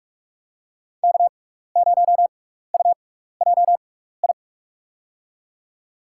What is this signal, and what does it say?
Morse code, heard as a keyed tone, about 41 wpm, spaced as for 15 wpm, 700 Hz. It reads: K0UJI
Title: USA Call Signs - 6x Character Spacing - 45wpm